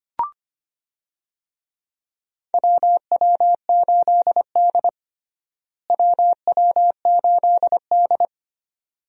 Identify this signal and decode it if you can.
Morse code, heard as a keyed tone, about 25 wpm, 700 Hz. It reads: WW8D WW8D
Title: USA Call Signs - 4 Characters - Repeated 2x - 25wpm